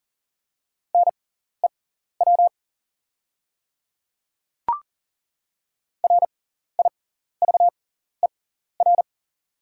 Morse code, heard as a keyed tone, about 40 wpm, spaced as for 14 wpm, 690 Hz. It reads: NEW RIVER
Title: 6x Character Spacing - Top 300 Words - 40wpm